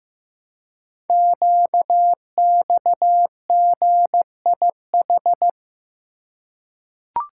Morse code, heard as a keyed tone, about 15 wpm, 700 Hz. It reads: QXGIH
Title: Code Groups - 5 Letters - 15wpm